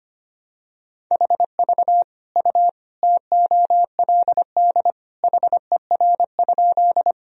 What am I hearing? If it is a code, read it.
Morse code, heard as a keyed tone, about 25 wpm, 700 Hz. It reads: HV U TOLD HER?
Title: Sentences from Top 400 Words and Abbreviations 25wpm